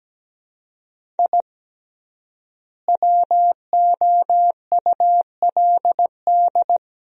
Morse code, heard as a keyed tone, about 17 wpm, 700 Hz. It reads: I WOULD
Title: 3x Word-Spacing - Sentences from Top 200 Words - 17wpm